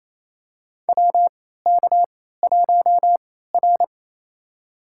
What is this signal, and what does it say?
Morse code, heard as a keyed tone, about 28 wpm, spaced as for 16 wpm, 700 Hz. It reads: WK1R
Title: USA Call Signs - 4 Characters - 3x Character Spacing - 28wpm